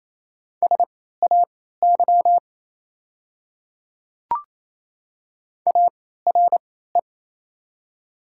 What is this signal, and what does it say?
Morse code, heard as a keyed tone, about 28 wpm, spaced as for 16 wpm, 700 Hz. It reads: SAY ARE
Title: 3x Character Spacing - Top 500 Words - 28wpm